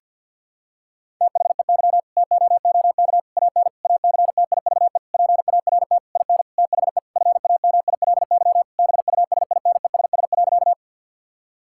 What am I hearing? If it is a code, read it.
Morse code, heard as a keyed tone, about 50 wpm, 700 Hz. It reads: THEY TOOK AN ACTIVE PART IN THE FAMILY BUSINESS.
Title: Sentences from Top 1000 Words - Review Part 6 - 50wpm